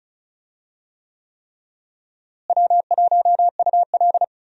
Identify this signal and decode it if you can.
Morse code, heard as a keyed tone, about 36 wpm, 700 Hz. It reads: W1UL
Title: USA Call Signs - 4 Characters - Repeated 3x - 35wpm